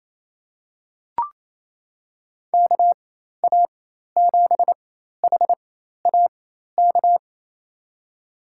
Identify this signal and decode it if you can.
Morse code, heard as a keyed tone, about 28 wpm, spaced as for 13 wpm, 700 Hz. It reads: KA7HAK
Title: USA Call Signs - 6 Characters - 4x Character Spacing - 28wpm